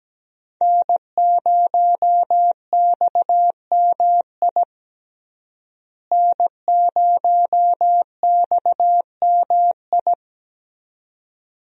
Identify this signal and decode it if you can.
Morse code, heard as a keyed tone, about 17 wpm, 700 Hz. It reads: N0XMI N0XMI
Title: USA Call Signs - 5 Characters - Repeated 2x - 17wpm